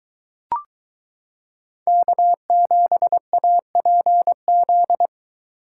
Morse code, heard as a keyed tone, about 22 wpm, 700 Hz. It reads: K7APZ